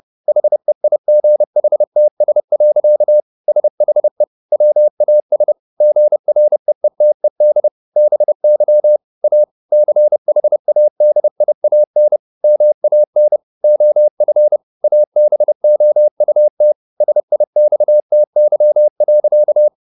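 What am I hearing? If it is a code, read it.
Morse code, heard as a keyed tone, about 30 wpm, 600 Hz. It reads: HEIGHTS. SHE WAS GREETED BY A CHADIAN MAN OF ABOUT SIXTY.
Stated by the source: Morse code at 30 wpm, the novel Never by Ken Follett in English, from CW med Gnister.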